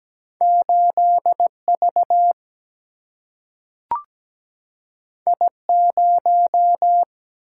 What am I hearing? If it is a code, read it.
Morse code, heard as a keyed tone, about 17 wpm, 700 Hz. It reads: 8 V I 0
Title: Single-Letter Single-Number 17wpm